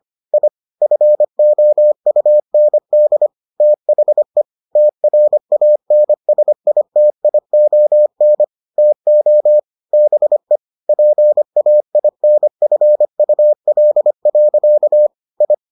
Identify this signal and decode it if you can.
Morse code, heard as a keyed tone, about 25 wpm, 600 Hz. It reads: I FOUND THE TRANSITION TO BE PAINFUL. I